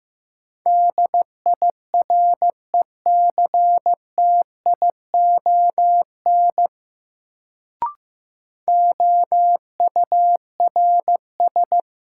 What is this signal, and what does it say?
Morse code, heard as a keyed tone, about 15 wpm, 700 Hz. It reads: DIRECTION OURS
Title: Sentences from Top 1000 Words - Review Part 5 - New Word Drill - 15wpm